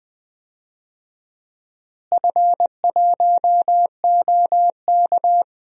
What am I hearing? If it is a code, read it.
Morse code, heard as a keyed tone, about 20 wpm, 700 Hz. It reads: F1OK
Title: DXCC-100 Call Signs - 4 Characters - Repeated 3x - 20wpm